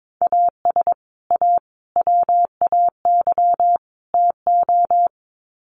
Morse code, heard as a keyed tone, about 22 wpm, 700 Hz. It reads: AS A WAY TO
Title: Sets of 4 Words 22wpm